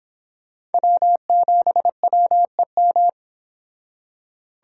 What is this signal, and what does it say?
Morse code, heard as a keyed tone, about 31 wpm, 700 Hz. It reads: W7WEM